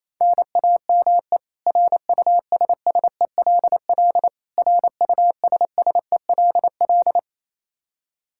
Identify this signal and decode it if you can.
Morse code, heard as a keyed tone, about 27 wpm, 700 Hz. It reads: NAME RUSSELL RUSSELL
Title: K1USN SST QSO Practice 28wpm